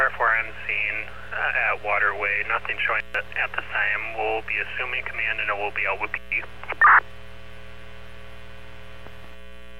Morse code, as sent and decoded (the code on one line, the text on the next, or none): none